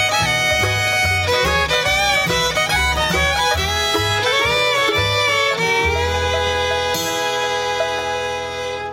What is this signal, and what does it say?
Dial-up modem handshake.